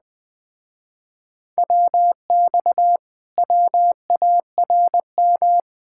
Morse code, heard as a keyed tone, about 20 wpm, 700 Hz. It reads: WX WARM
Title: Speed-Racing Common QSO Phrases Encore - 17wpm